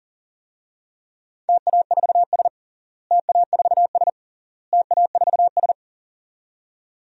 Morse code, heard as a keyed tone, about 40 wpm, 700 Hz. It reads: TA4S TA4S TA4S
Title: DXCC-100 Call Signs - 4 Characters - Repeated 3x - 40wpm